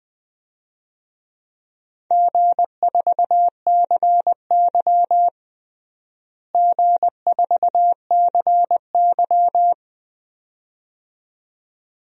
Morse code, heard as a keyed tone, about 20 wpm, 700 Hz. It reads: G4CY G4CY
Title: DXCC-100 Call Signs - 4 Characters - Repeated 2x - 20wpm